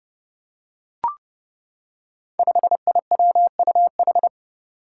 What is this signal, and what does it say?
Morse code, heard as a keyed tone, about 30 wpm, 700 Hz. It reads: 5IWUH
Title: Five Challenging Letter-Number Combinations 30wpm